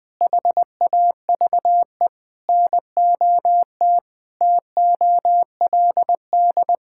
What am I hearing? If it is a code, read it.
Morse code, heard as a keyed tone, about 20 wpm, 700 Hz. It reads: HAVE NOT TOLD
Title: Sentences from Top 1000 Words - Review Part 5 - 20wpm